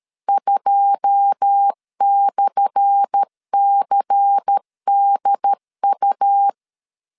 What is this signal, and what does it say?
Morse code, heard as a keyed tone, about 13 wpm, 800 Hz. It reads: 2/CDU
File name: LEZIONI / 10 SIGNS / 10C DOT 12wpm 2451